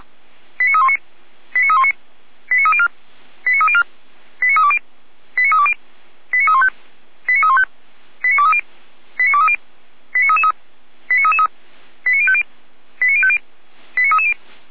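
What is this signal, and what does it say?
Five-tone selective call (ZVEI), heard as quick runs of five tones, tones of about 70 ms. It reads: 89319 89319 89304 89304 89320 89320 89315 89315 89230 89230 89333 89333 89050 89050 89330
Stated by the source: ZVEI five-tone selective call 89110 received by SDR.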